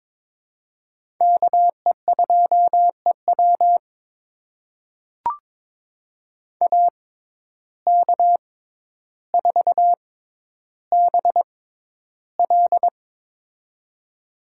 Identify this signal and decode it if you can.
Morse code, heard as a keyed tone, about 22 wpm, spaced as for 11 wpm, 700 Hz. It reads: KE2EW AK4BL
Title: USA Call Signs - 5 Characters - 6x Character Spacing - 22wpm